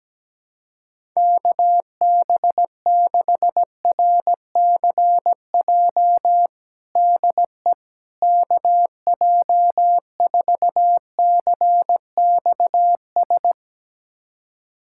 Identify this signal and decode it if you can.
Morse code, heard as a keyed tone, about 17 wpm, 700 Hz. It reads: KB6RCJ DE KJ4CXS
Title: USA Callsign DE Callsign - 6 Characters - 17wpm